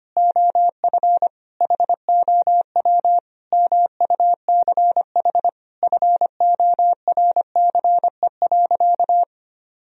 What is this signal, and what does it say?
Morse code, heard as a keyed tone, about 25 wpm, 700 Hz. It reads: OF HOW MUCH FORCE.
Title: Sentences from Top 1000 Words - Review Part 3 - 25wpm